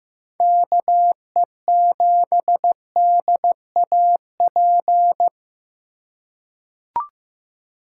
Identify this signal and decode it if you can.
Morse code, heard as a keyed tone, about 15 wpm, 700 Hz. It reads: KE7DAP
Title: USA Call Signs - 2x Character Spacing - 15wpm